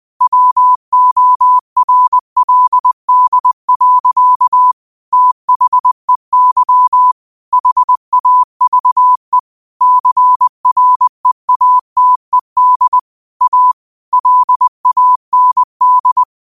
Morse code, heard as a keyed tone, about 20 wpm, 1000 Hz. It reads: WORLD. THEY HAVE CREATED A LAND
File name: qotdfiles / 20wpm